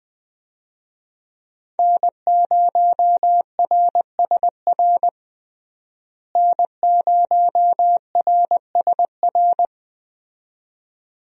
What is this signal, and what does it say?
Morse code, heard as a keyed tone, about 20 wpm, 700 Hz. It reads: N0RSR N0RSR
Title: USA Call Signs - 5 Characters - Repeated 2x - 20wpm